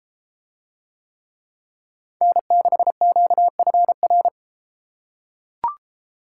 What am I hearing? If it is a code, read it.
Morse code, heard as a keyed tone, about 31 wpm, 700 Hz. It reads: N6QFR